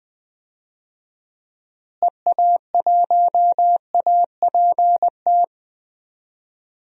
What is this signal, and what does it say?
Morse code, heard as a keyed tone, about 20 wpm, 700 Hz. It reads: EA1APT
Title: DXCC-100 Call Signs - 6 Characters - Repeated 3x - 20wpm